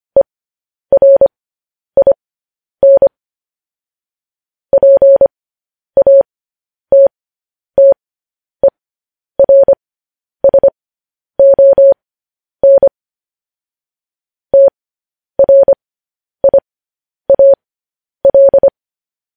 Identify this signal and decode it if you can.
Morse code, heard as a keyed tone, about 25 wpm, spaced as for 10 wpm, 550 Hz. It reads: ERIN PATTERSON TRIAL
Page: news headlines in morse code at 10 wpm